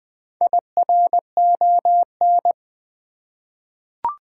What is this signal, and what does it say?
Morse code, heard as a keyed tone, about 20 wpm, 700 Hz. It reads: IRON
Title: Four Letter Words - 20wpm